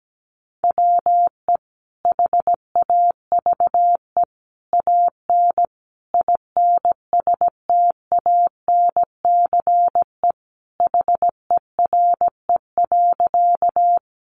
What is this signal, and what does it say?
Morse code, heard as a keyed tone, about 17 wpm, 700 Hz. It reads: WE HAVE AN INSTANCE HERE.
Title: Sentences from Top 800 Words 17wpm